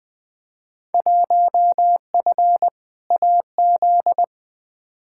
Morse code, heard as a keyed tone, about 20 wpm, 700 Hz. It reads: 1F AZ
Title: Field Day Exchanges - 20wpm